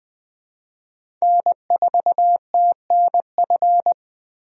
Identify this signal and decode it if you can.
Morse code, heard as a keyed tone, about 20 wpm, 700 Hz. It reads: N4TNF
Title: USA Call Signs - 5 Characters - 20wpm